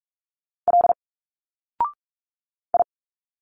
Morse code, heard as a keyed tone, about 46 wpm, 700 Hz. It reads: L I
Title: Single Letters 45wpm